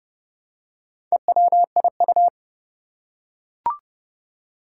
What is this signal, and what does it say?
Morse code, heard as a keyed tone, about 31 wpm, 700 Hz. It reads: EWIU
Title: Four Challenging Letter-Number Combinations 30wpm